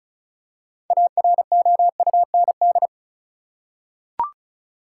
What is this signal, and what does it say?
Morse code, heard as a keyed tone, about 36 wpm, 700 Hz. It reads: AROUND
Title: Six Letter Words - 35wpm